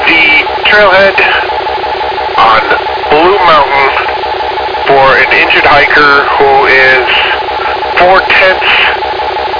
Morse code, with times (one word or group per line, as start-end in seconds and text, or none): none